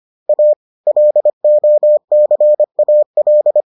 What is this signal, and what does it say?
Morse code, heard as a keyed tone, about 25 wpm, 600 Hz. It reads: A LOCAL